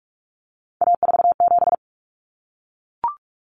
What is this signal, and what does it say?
Morse code, heard as a keyed tone, about 44 wpm, 700 Hz. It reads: A47
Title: Common Cut Numbers - 3-Digit 45wpm